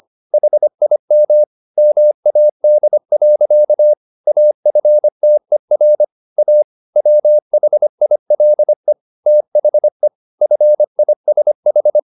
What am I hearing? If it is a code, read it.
Morse code, heard as a keyed tone, about 25 wpm, 600 Hz. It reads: HIM MAD. AFTER A WHILE THE FISH